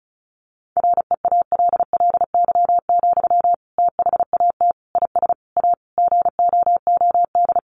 Morse code, heard as a keyed tone, about 34 wpm, 700 Hz. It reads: REALLY, THAT IS A GOOD
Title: Sentences from Top 600 Words 35wpm